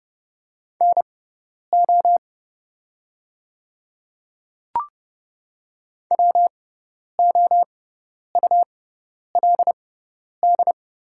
Morse code, heard as a keyed tone, about 30 wpm, spaced as for 10 wpm, 700 Hz. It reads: NO WOULD
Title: 6x Character Spacing - Top 300 Words - 30wpm